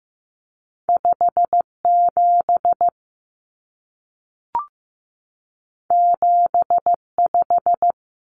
Morse code, heard as a keyed tone, about 15 wpm, 700 Hz. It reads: 57 75
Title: Common Cut Numbers - 2-Digit 15wpm